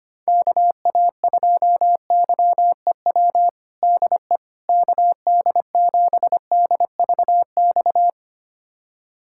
Morse code, heard as a keyed tone, about 25 wpm, 700 Hz. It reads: KA2YEW DE KD7DVX
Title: USA Callsign DE Callsign - 6 Characters - 25wpm